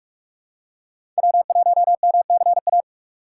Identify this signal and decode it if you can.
Morse code, heard as a keyed tone, about 41 wpm, 690 Hz. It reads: W1MCA
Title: USA Call Signs - 5 Characters - 6x Character Spacing - 45wpm